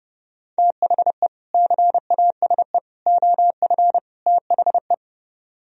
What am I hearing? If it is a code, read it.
Morse code, heard as a keyed tone, about 35 wpm, 700 Hz. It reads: THE CASE OF THE
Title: Speed-Racing - Sets of 4 Words 30wpm